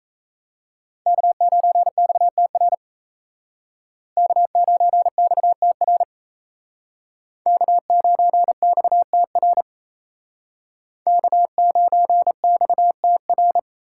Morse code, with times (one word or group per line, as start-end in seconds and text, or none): 1.1-2.7 K9XTR
4.2-6.0 K9XTR
7.5-9.6 K9XTR
11.1-13.6 K9XTR